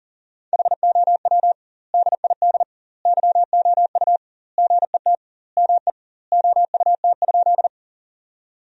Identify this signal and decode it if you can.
Morse code, heard as a keyed tone, about 40 wpm, 700 Hz. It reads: HOW DID YOU GET ME OUT?
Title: 2x Word-Spacing - Sentences from Top 200 Words - 40wpm